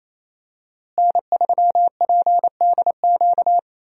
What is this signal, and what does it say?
Morse code, heard as a keyed tone, about 27 wpm, 700 Hz. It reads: N3PDQ
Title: USA Call Signs - 5 Characters - Repeated 2x - 28wpm